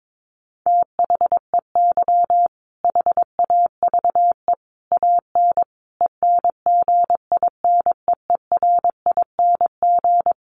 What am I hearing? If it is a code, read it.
Morse code, heard as a keyed tone, about 22 wpm, 700 Hz. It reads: THEY HAVE AN ENGINEERING
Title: Sentences from Top 1500 Words 22wpm